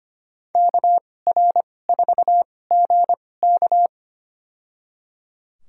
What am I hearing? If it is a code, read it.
Morse code, heard as a keyed tone, about 25 wpm, spaced as for 18 wpm, 700 Hz. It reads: KR4GK
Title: USA Call Signs - 2x Character Spacing - 25wpm